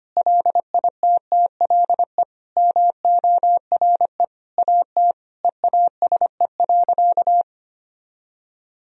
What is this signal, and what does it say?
Morse code, heard as a keyed tone, about 25 wpm, 700 Hz. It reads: LITTLE MORE AT EASE.
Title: Sentences from Top 1000 Words - Review Part 1 - 25wpm